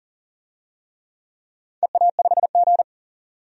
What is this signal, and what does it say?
Morse code, heard as a keyed tone, about 41 wpm, 700 Hz. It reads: EA5G
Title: DXCC-100 Call Signs - 4 Characters - 40wpm